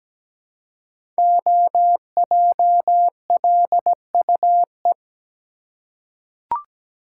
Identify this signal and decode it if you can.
Morse code, heard as a keyed tone, about 17 wpm, 700 Hz. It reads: OJLUE
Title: Code Groups - 5 Letter:Numbers - 17wpm